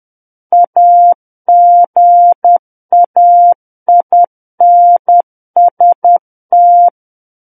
Morse code, heard as a keyed tone, about 10 wpm, 700 Hz. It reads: AGAINST
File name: Book One / 01-Chapter 1 - 10 WPM